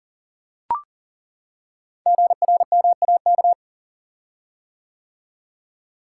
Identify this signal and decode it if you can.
Morse code, heard as a keyed tone, about 41 wpm, 700 Hz. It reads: GRMAK
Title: Code Groups - 5 Letters - 40wpm